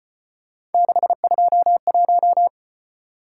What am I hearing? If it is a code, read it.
Morse code, heard as a keyed tone, about 33 wpm, 700 Hz. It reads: 621